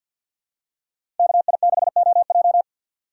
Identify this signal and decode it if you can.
Morse code, heard as a keyed tone, about 46 wpm, 700 Hz. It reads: KI6OJ
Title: USA Call Signs - 5 Characters - 4x Character Spacing - 50wpm